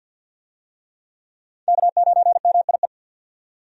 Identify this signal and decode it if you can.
Morse code, heard as a keyed tone, about 50 wpm, 700 Hz. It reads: K9MIE
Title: USA Call Signs - 5 Characters - 50wpm